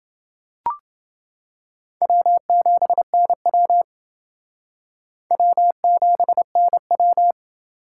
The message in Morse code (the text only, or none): W7NW W7NW